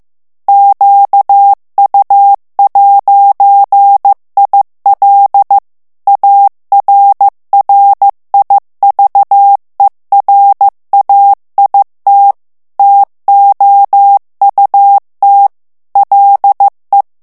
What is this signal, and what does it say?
Morse code, heard as a keyed tone, about 15 wpm, 800 Hz. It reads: QU'IL ARRIVERAIT TOUT LE
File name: morse mp3 / FRANCAIS TEXTE CLAIR 2 VITESSE 14 MOTS MINUTES